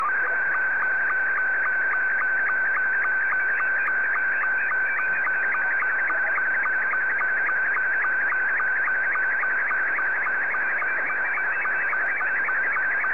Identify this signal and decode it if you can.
SSTV (slow-scan television), Scottie 2.